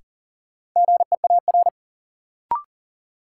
Morse code, heard as a keyed tone, about 38 wpm, 700 Hz. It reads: GEAR
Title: Sentences from Top 1400 Words - New Word Drill Alt - 40wpm